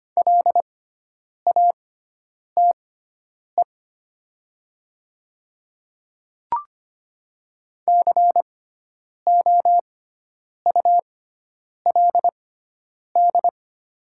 Morse code, heard as a keyed tone, about 25 wpm, spaced as for 9 wpm, 700 Hz. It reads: LATE COULD
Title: 6x Character Spacing - Top 300 Words - 25wpm